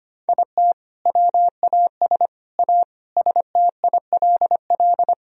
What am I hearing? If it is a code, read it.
Morse code, heard as a keyed tone, about 25 wpm, 700 Hz. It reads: IT WAS A STILL
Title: Sentences from Top 1000 Words - Review Part 2 - 25wpm